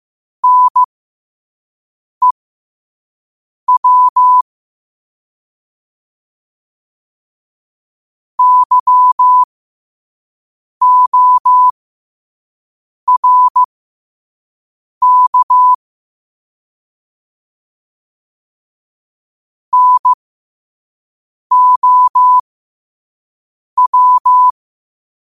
Morse code, heard as a keyed tone, about 15 wpm, spaced as for 5 wpm, 1000 Hz. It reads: NEW YORK NOW
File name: 5wpm